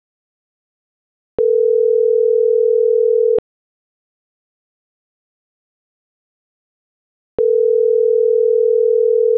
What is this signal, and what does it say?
Ringback tone (North America), 440 + 480 Hz.